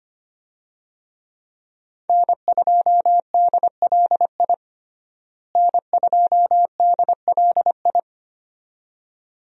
Morse code, heard as a keyed tone, about 25 wpm, 700 Hz. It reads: N2DLI N2DLI
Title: USA Call Signs - 5 Characters - Repeated 2x - 25wpm